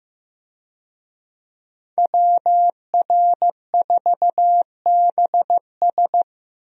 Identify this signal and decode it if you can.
Morse code, heard as a keyed tone, about 15 wpm, 700 Hz. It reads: WR4BS
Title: USA Call Signs - 5 Characters - 15wpm